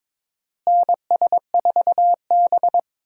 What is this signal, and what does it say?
Morse code, heard as a keyed tone, about 22 wpm, 700 Hz. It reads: NS4B
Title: USA Call Signs - 4 Characters - Repeated 3x - 22wpm